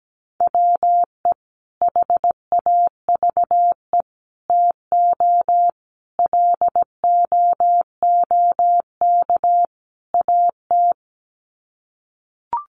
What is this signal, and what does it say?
Morse code, heard as a keyed tone, about 17 wpm, 700 Hz. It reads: WE HAVE TO LOOK AT
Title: Sets of 5 Words Encore2 17wpm